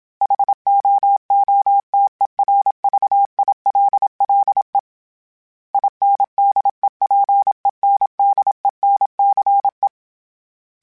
Morse code, heard as a keyed tone, about 27 wpm, 800 Hz. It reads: HOOTERVILLE INDEPENDENCE
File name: fromSMs / FromSMs mp3s / Cities@25